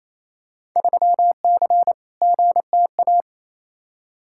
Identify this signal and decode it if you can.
Morse code, heard as a keyed tone, about 28 wpm, 700 Hz. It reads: 3C GTA
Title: Field Day Exchanges Encore - 28wpm